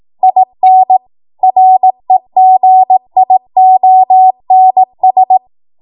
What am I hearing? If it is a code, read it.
Morse code, heard as a keyed tone, about 18 wpm, 750 Hz. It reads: IN REGIONS
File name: Morse / 18WPM